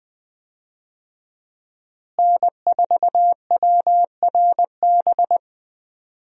Morse code, heard as a keyed tone, about 20 wpm, 700 Hz. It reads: N4WRB
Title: USA Call Signs - 5 Characters - Repeated 3x - 20wpm